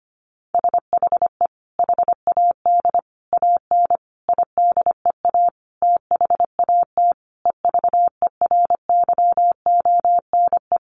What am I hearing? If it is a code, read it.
Morse code, heard as a keyed tone, about 25 wpm, 700 Hz. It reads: SHE HAD AN IDEA THAT EVERYONE
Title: Sentences from Top 1400 Words 25wpm